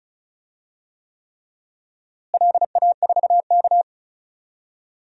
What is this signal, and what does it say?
Morse code, heard as a keyed tone, about 34 wpm, 700 Hz. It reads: LA4K
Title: DXCC-100 Call Signs - 4 Characters - Repeated 3x - 35wpm